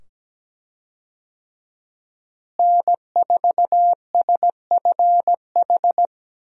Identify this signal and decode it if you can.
Morse code, heard as a keyed tone, about 17 wpm, 700 Hz. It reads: N4SFH
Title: USA Call Signs - 5 Characters - Repeated 2x - 17wpm